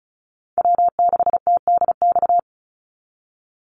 Morse code, heard as a keyed tone, about 34 wpm, 700 Hz. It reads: W6TDX
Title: NA SOTA Chaser Callsigns 35wpm